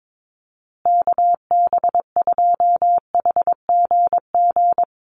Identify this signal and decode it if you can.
Morse code, heard as a keyed tone, about 22 wpm, 700 Hz. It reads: KB2HGG
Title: USA Call Signs Repeated 2x 22wpm